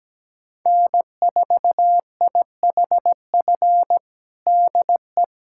Morse code, heard as a keyed tone, about 17 wpm, 700 Hz. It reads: N4IHF DE
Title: USA Callsign DE Callsign - 5 Characters - 17wpm